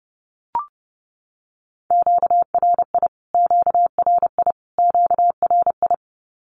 Morse code, heard as a keyed tone, about 30 wpm, 700 Hz. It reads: QRI QRI QRI
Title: Q-Codes Repeated 3x 30wpm